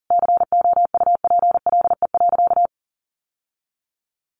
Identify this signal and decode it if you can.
Morse code, heard as a keyed tone, about 40 wpm, 700 Hz. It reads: COUPLE.
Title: Sentences from Top 1200 Words 40wpm